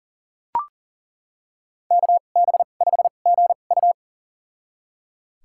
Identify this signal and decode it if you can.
Morse code, heard as a keyed tone, about 40 wpm, spaced as for 29 wpm, 700 Hz. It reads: KB5GU
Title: USA Call Signs - 2x Character Spacing - 40wpm